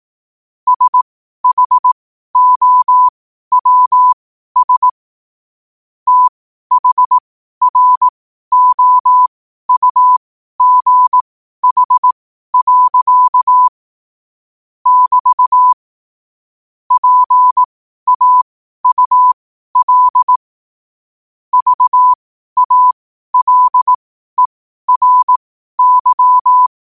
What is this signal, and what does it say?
Morse code, heard as a keyed tone, about 18 wpm, spaced as for 12 wpm, 1000 Hz. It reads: SHOWS THROUGH. <BT> PAUL VALERY
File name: qotdfiles / 12wpm